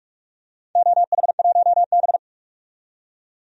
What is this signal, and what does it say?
Morse code, heard as a keyed tone, about 46 wpm, 700 Hz. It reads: OH1B
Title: DXCC-100 Call Signs - 4 Characters - 45wpm